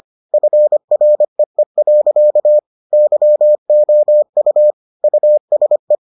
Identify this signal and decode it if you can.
Morse code, heard as a keyed tone, about 25 wpm, 600 Hz. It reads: FREE. YOU USE